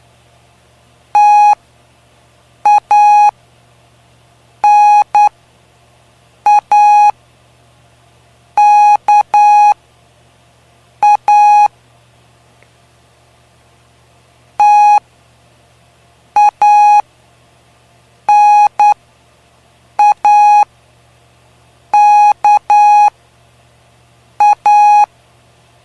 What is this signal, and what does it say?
Morse code, heard as a keyed tone, about 9 wpm, spaced as for 5 wpm, 830 Hz. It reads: TANAKA TANAKA
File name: cw / py2lcd / 05-PY2LCD-Li��o 5